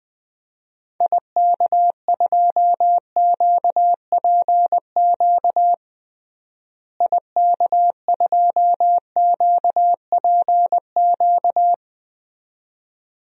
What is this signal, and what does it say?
Morse code, heard as a keyed tone, about 20 wpm, 700 Hz. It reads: IK2QPQ IK2QPQ
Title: DXCC-100 Call Signs - 6 Characters - Repeated 2x - 20wpm